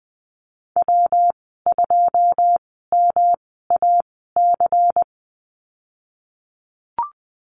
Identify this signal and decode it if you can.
Morse code, heard as a keyed tone, about 20 wpm, spaced as for 15 wpm, 700 Hz. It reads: W2MAC
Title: 147 Lesson 24g - Letter y - Call Signs 20wpm